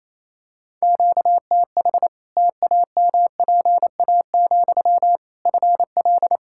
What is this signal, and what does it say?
Morse code, heard as a keyed tone, about 28 wpm, 700 Hz. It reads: QTH TAMPA, FL
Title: Practice QSO - Exchange 1 - Beginning - 28wpm